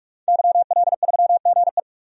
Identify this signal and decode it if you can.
Morse code, heard as a keyed tone, about 47 wpm, 700 Hz. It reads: YL3GE
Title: DXCC-100 Call Signs - 5 Characters - Repeated 3x - 45wpm